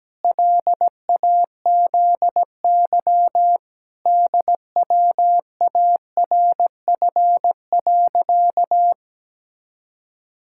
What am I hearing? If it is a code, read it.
Morse code, heard as a keyed tone, about 17 wpm, 700 Hz. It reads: LAZY DWARF.
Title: Pangram Challenge v2 - 17wpm